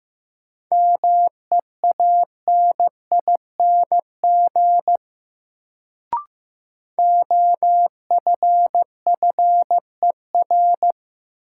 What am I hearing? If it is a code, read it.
Morse code, heard as a keyed tone, about 15 wpm, 700 Hz. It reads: MEANING OFFER
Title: Sentences from Top 900 Words - New Word Drill - 15wpm